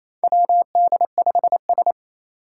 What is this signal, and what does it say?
Morse code, heard as a keyed tone, about 27 wpm, 700 Hz. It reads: WD5S
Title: Four Challenging Letter-Number Combinations 28wpm